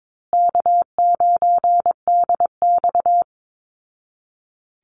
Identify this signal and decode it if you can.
Morse code, heard as a keyed tone, about 22 wpm, 700 Hz. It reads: K9DX
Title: Call Signs Repeated 3x - 22wpm